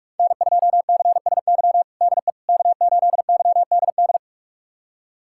Morse code, heard as a keyed tone, about 42 wpm, 700 Hz. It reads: N2CSY DE K8YDD
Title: USA Callsign DE Callsign - 5 Characters - 45wpm